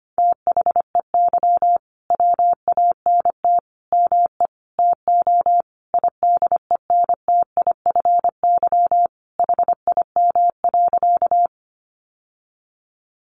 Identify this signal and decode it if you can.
Morse code, heard as a keyed tone, about 25 wpm, 700 Hz. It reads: THEY WANT ME TO IDENTIFY HIM.
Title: Sentences from Top 900 Words 25wpm